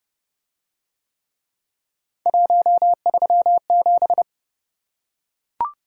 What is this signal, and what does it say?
Morse code, heard as a keyed tone, about 30 wpm, 700 Hz. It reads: 137